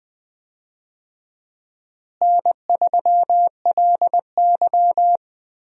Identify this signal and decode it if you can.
Morse code, heard as a keyed tone, about 20 wpm, 700 Hz. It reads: N3LY
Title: USA Call Signs - 4 Characters - Repeated 2x - 20wpm